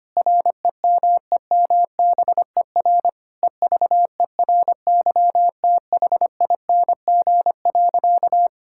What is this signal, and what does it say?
Morse code, heard as a keyed tone, about 25 wpm, 700 Hz. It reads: REMEMBER EVERYTHING.
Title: Sentences from Top 1000 Words - Review Part 5 - 25wpm